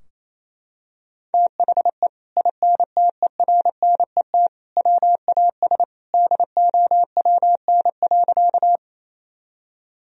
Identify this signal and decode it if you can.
Morse code, heard as a keyed tone, about 28 wpm, 700 Hz. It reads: THE INTERNET WAS DOWN.
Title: Sentences from Top 1000 Words - Review Part 4 - 28wpm